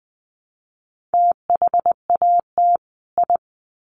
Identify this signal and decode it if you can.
Morse code, heard as a keyed tone, about 20 wpm, 700 Hz. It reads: THAT I
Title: Sets of 2 Words 20wpm